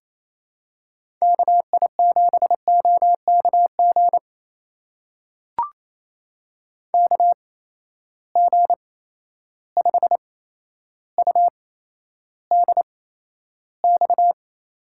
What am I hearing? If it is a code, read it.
Morse code, heard as a keyed tone, about 28 wpm, spaced as for 12 wpm, 700 Hz. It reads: KI7OKG KG5UDX
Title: USA Call Signs - 6 Characters - 8x Character Spacing - 28wpm